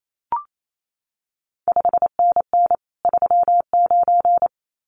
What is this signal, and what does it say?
Morse code, heard as a keyed tone, about 28 wpm, 700 Hz. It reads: 5NN 39
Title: CQ WW DX Contest Exchange - 28wpm